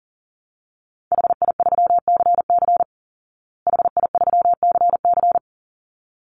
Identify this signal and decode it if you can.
Morse code, heard as a keyed tone, about 40 wpm, 700 Hz. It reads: HI3CC HI3CC
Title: Call Signs Repeated 3x - 40wpm